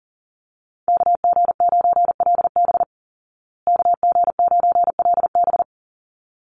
Morse code, heard as a keyed tone, about 40 wpm, 700 Hz. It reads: KG9LB KG9LB
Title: Call Signs Repeated 3x - 40wpm